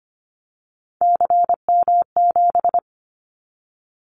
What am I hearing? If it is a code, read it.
Morse code, heard as a keyed tone, about 25 wpm, 700 Hz. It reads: CM7
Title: Callsign Prefixes for Top 100 DXCC - 25wpm